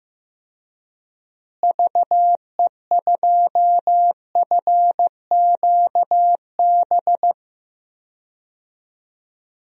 Morse code, heard as a keyed tone, about 15 wpm, 700 Hz. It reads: VE2FQB
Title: DXCC-100 Call Signs - 6 Characters - Repeated 2x - 15wpm